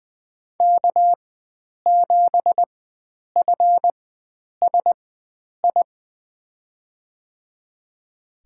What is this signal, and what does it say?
Morse code, heard as a keyed tone, about 20 wpm, spaced as for 9 wpm, 700 Hz. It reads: K7FSI
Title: USA Call Signs - 5 Characters - 4x Character Spacing - 20wpm